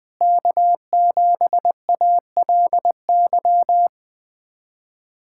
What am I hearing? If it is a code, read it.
Morse code, heard as a keyed tone, about 20 wpm, 700 Hz. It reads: K7ALY
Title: USA Callsign DE Callsign - 5 Characters - 20wpm